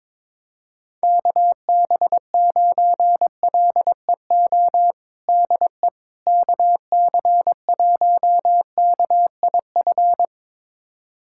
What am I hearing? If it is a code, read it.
Morse code, heard as a keyed tone, about 22 wpm, 700 Hz. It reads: KB9LEO DE KC1KIF
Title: USA Callsign DE Callsign - 6 Characters - 22wpm